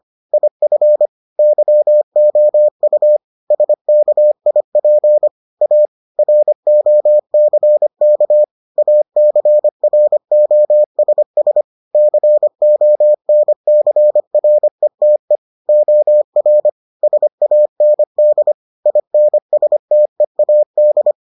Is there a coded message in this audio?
Yes: Morse code from 0.3 to 21.1 s, heard as a keyed tone, about 25 wpm, 600 Hz. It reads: IF YOU SKIP A ROCK ACROSS CONCRETE OR SAND INSTEAD